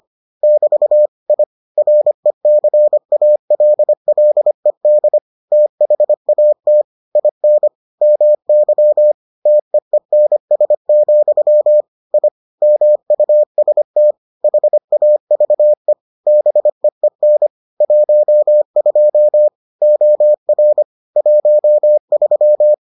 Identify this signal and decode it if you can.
Morse code, heard as a keyed tone, about 25 wpm, 600 Hz. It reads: <BT> I RECALLED THAT IN MY TEENS, I MUST HAVE BEEN 12 OR 13